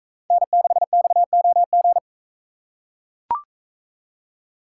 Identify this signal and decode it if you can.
Morse code, heard as a keyed tone, about 42 wpm, 700 Hz. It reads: N6XOG